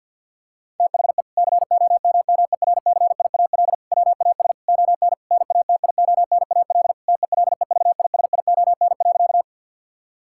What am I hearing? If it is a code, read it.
Morse code, heard as a keyed tone, about 49 wpm, 700 Hz. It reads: THE COMMERCIAL WAS ON NATIONAL TELEVISION.
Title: Sentences from Top 1000 Words - Review Part 5 - 50wpm